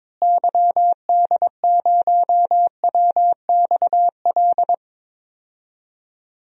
Morse code, heard as a keyed tone, about 22 wpm, 700 Hz. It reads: YD0WXL
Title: DXCC-100 Call Signs - 6 Characters - Repeated 2x - 22wpm